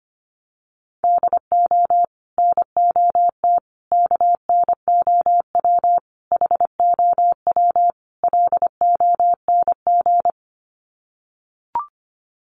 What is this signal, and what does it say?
Morse code, heard as a keyed tone, about 25 wpm, 700 Hz. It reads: DO NOT KNOW HOW LONG
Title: Sets of 5 Words Encore2 25wpm